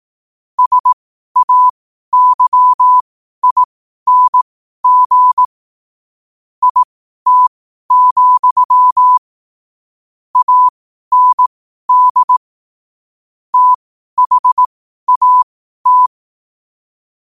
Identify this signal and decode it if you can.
Morse code, heard as a keyed tone, about 18 wpm, spaced as for 12 wpm, 1000 Hz. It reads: SAYING IT, AND THAT
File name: qotdfiles / 12wpm